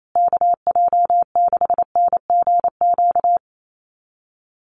Morse code, heard as a keyed tone, about 28 wpm, 700 Hz. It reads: KJ6NGQ
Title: USA Call Signs Repeated 2x 28wpm